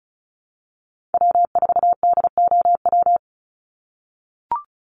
Morse code, heard as a keyed tone, about 34 wpm, 700 Hz. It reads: W4DOW
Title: NA SOTA Chaser Callsigns 35wpm